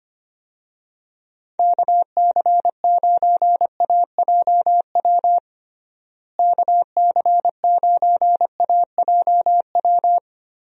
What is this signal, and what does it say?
Morse code, heard as a keyed tone, about 25 wpm, 700 Hz. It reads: KC9AJW KC9AJW
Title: USA Call Signs - 6 Characters - Repeated 3x - 25wpm